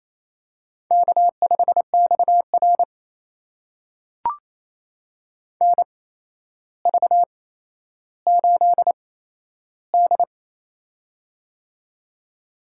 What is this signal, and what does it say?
Morse code, heard as a keyed tone, about 27 wpm, spaced as for 12 wpm, 700 Hz. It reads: K5XR NV8D
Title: USA Call Signs - 4 Characters - 8x Character Spacing - 28wpm